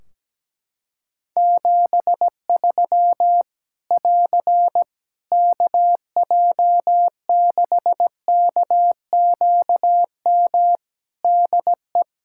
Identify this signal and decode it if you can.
Morse code, heard as a keyed tone, about 17 wpm, 700 Hz. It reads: 73 <AR> KJ6KQM DE
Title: Practice QSO - Exchange 4 - Ending - 17wpm